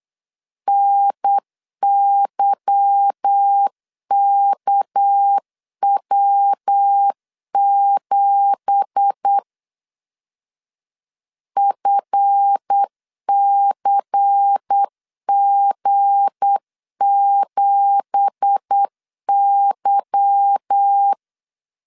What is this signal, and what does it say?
Morse code, heard as a keyed tone, about 8 wpm, 800 Hz. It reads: NYKW7 FCG7Y
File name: 8wpm